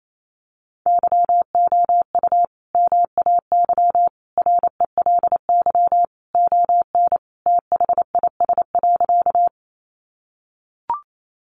Morse code, heard as a keyed tone, about 28 wpm, 700 Hz. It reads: YOU MAY RELY ON THIS.
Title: Sentences from Top 1200 Words 28wpm